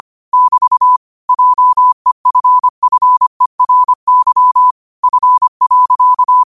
Morse code, heard as a keyed tone, about 25 wpm, 1000 Hz. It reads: <BT> JEFFERY F.
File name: qotdfiles / 25wpm